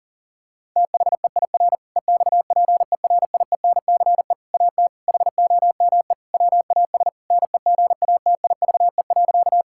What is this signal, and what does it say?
Morse code, heard as a keyed tone, about 40 wpm, 700 Hz. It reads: THEIR EXPERIENCE AT HOME WAS NEGATIVE.
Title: Sentences from Top 1000 Words - Review Part 6 - 40wpm